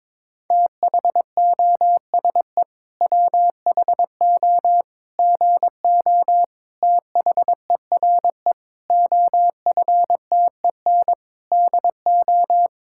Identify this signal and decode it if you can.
Morse code, heard as a keyed tone, about 22 wpm, 700 Hz. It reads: THOSE WHO GO THERE OFTEN DO
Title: Sentences from Top 1000 Words - Review Part 5 - 22wpm